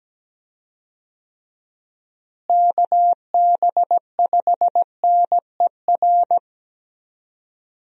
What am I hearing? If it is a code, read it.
Morse code, heard as a keyed tone, about 17 wpm, 700 Hz. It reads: KB5NER
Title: USA Call Signs - 6 Characters - Repeated 2x - 17wpm